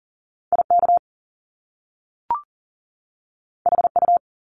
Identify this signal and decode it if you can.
Morse code, heard as a keyed tone, about 38 wpm, 700 Hz. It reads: IK HU
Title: Top Two Letter Combinations in Words 40wpm